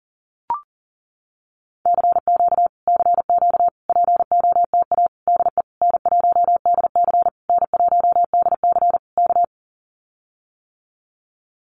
Morse code, heard as a keyed tone, about 40 wpm, 700 Hz. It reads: CQ CQ POTA DE N1DC N1DC K
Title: Common QSO Phrases Encore 40wpm